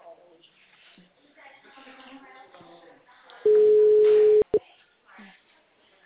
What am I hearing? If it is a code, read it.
Ringback tone (Europe, 425 Hz), with speech.
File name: out- -2004- - - .2313